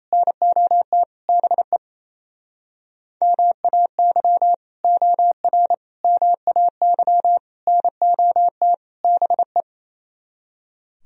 Morse code, heard as a keyed tone, about 28 wpm, 700 Hz. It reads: NOT BE MAY OR MAY NOT BE
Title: Speed-Racing - Sets of 5 Words 28wpm